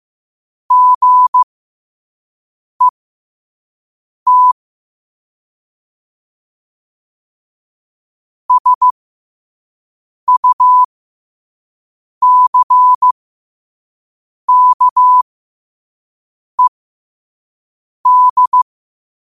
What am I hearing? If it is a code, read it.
Morse code, heard as a keyed tone, about 15 wpm, spaced as for 5 wpm, 1000 Hz. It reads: GET SUCKED
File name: qotdfiles / 5wpm